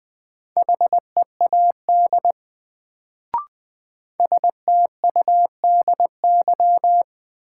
Morse code, heard as a keyed tone, about 21 wpm, 700 Hz. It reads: HEAD STUDY
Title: Sentences from Top 300 Words - New Word Drill - 20wpm